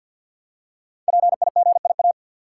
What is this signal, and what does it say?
Morse code, heard as a keyed tone, about 50 wpm, 700 Hz. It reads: PIGIA
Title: Code Groups - 5 Letters - 50wpm